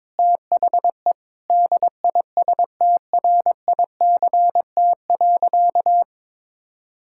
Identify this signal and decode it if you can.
Morse code, heard as a keyed tone, about 22 wpm, 700 Hz. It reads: THE DISTRICT.
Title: Sentences from Top 1600 Words 22wpm